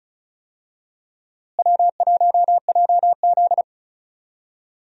Morse code, heard as a keyed tone, about 36 wpm, 700 Hz. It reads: W1JZ
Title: USA Call Signs - 4 Characters - Repeated 3x - 35wpm